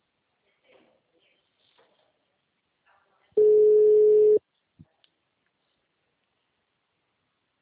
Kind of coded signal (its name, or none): telephone call progress tone